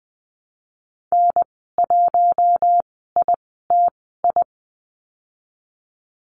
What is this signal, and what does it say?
Morse code, heard as a keyed tone, about 20 wpm, spaced as for 14 wpm, 700 Hz. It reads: N1ITI
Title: 042 Lesson 8g - Number 1 - Call Signs 20wpm